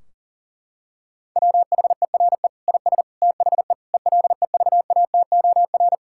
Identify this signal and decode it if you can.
Morse code, heard as a keyed tone, about 40 wpm, 700 Hz. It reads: WHERE IS THE ELEVATOR
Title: Sentences from Top 1600 Words 40wpm